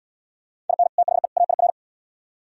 Morse code, heard as a keyed tone, about 76 wpm, 700 Hz. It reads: IN THE AIR